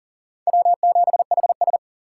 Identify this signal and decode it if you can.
Morse code, heard as a keyed tone, about 52 wpm, 700 Hz. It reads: W7HS